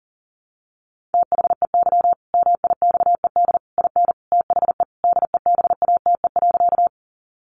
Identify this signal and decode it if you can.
Morse code, heard as a keyed tone, about 40 wpm, 700 Hz. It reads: THEY MIXED IN THE DEBATE.
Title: Sentences from Top 1500 Words 40wpm